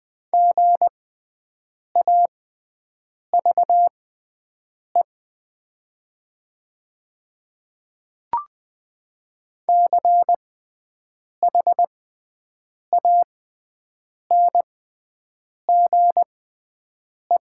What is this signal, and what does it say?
Morse code, heard as a keyed tone, about 20 wpm, spaced as for 7 wpm, 700 Hz. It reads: GAVE CHANGE
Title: 6x Character Spacing - Top 500 Words - 20wpm